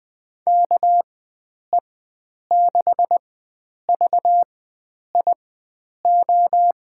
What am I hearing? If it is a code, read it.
Morse code, heard as a keyed tone, about 20 wpm, spaced as for 9 wpm, 700 Hz. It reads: KE6VIO
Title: USA Call Signs - 4x Character Spacing - 20wpm